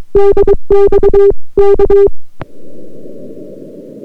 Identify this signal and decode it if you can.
Morse code, heard as a keyed tone, about 22 wpm, 400 Hz. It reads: DX K